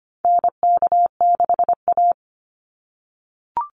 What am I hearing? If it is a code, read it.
Morse code, heard as a keyed tone, about 25 wpm, 700 Hz. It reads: NK6A
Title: NA SOTA Chaser Callsigns 25wpm